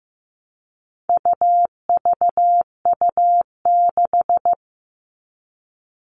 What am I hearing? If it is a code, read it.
Morse code, heard as a keyed tone, about 15 wpm, 700 Hz. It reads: UVU6
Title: All Cut Numbers - 4-Digit 15wpm